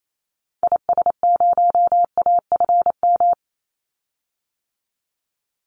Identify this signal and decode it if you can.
Morse code, heard as a keyed tone, about 28 wpm, 700 Hz. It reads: IS0AFM
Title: Call Signs - 28wpm